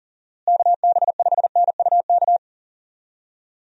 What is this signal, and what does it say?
Morse code, heard as a keyed tone, about 39 wpm, 700 Hz. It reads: KB5NUK
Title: USA Call Signs - 6 Characters - Repeated 2x - 40wpm